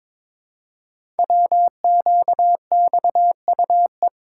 Morse code, heard as a keyed tone, about 22 wpm, 700 Hz. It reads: WQXUE
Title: Code Groups - 5 Letters - 22wpm